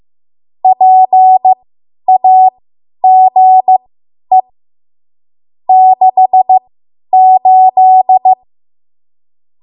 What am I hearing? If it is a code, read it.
Morse code, heard as a keyed tone, about 15 wpm, spaced as for 10 wpm, 750 Hz. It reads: PAGE 68
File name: Archive / 10 WPM / 10WPM